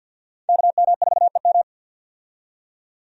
Morse code, heard as a keyed tone, about 47 wpm, 700 Hz. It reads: KM4EM
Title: USA Call Signs - 5 Characters - 50wpm